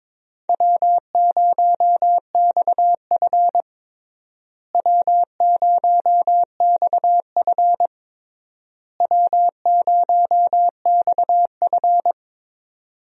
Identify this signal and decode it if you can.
Morse code, heard as a keyed tone, about 22 wpm, 700 Hz. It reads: W0XF W0XF W0XF
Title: USA Call Signs - 4 Characters - Repeated 3x - 22wpm